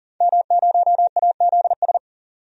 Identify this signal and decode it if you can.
Morse code, heard as a keyed tone, about 40 wpm, 700 Hz. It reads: M0AZS
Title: DXCC-100 Call Signs - 5 Characters - 40wpm